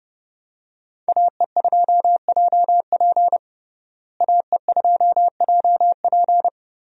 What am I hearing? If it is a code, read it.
Morse code, heard as a keyed tone, about 30 wpm, 700 Hz. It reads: AE2JP AE2JP
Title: USA Call Signs - 5 Characters - Repeated 3x - 30wpm